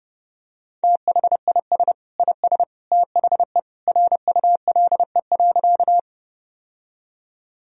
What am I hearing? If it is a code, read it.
Morse code, heard as a keyed tone, about 29 wpm, 700 Hz. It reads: THIS IS THE RULE.
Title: Sentences from Top 1000 Words - Review Part 2 - 30wpm